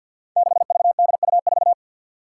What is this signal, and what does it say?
Morse code, heard as a keyed tone, about 49 wpm, 700 Hz. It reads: 6VDU4
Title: Five Challenging Letter-Number Combinations 50wpm